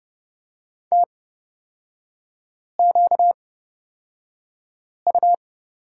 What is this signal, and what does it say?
Morse code, heard as a keyed tone, about 31 wpm, spaced as for 5 wpm, 700 Hz. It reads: TQU
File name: Lesson 38e - Letter q - Three Characters 30wpm